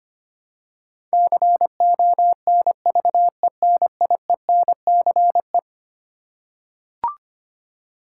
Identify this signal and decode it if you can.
Morse code, heard as a keyed tone, about 25 wpm, 700 Hz. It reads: CONVENIENCE
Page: morse-code-ninja-practice-25wpm